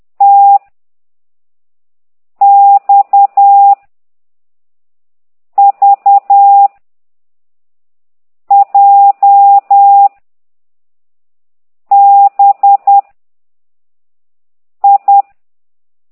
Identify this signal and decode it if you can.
Morse code, heard as a keyed tone, about 10 wpm, 800 Hz. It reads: T X V J B I
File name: ct2idw / exerc9